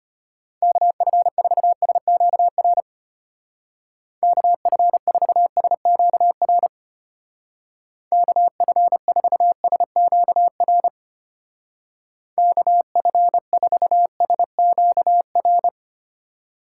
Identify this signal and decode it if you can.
Morse code, heard as a keyed tone, about 32 wpm, 700 Hz. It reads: KF4SQR KF4SQR KF4SQR KF4SQR